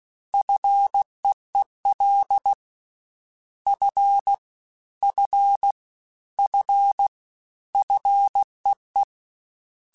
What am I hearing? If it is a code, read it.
Morse code, heard as a keyed tone, about 16 wpm, 800 Hz. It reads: FEEL F F F FEE
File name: F and L / F and L mp3 / F and L Prefixes 3x2 Twice @15